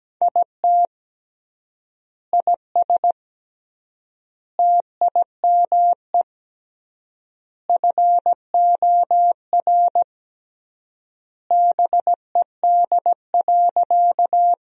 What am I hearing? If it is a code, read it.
Morse code, heard as a keyed tone, about 17 wpm, 700 Hz. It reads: IT IS TIME FOR BED.
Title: 3x Word-Spacing - Sentences from Top 500 Words - 17wpm